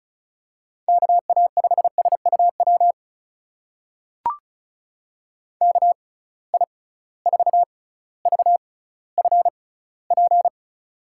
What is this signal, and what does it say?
Morse code, heard as a keyed tone, about 36 wpm, spaced as for 18 wpm, 700 Hz. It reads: KA5SUW KI4VFP
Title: USA Call Signs - 6 Characters - 6x Character Spacing - 35wpm